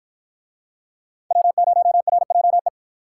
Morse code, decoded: W0RJE

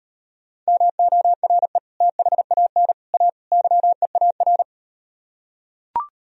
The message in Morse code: MORE THAN A YEAR

